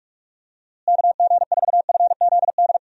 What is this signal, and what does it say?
Morse code, heard as a keyed tone, about 42 wpm, 700 Hz. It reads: KG4FZD
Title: USA Call Signs - 6 Characters - Repeated 2x - 45wpm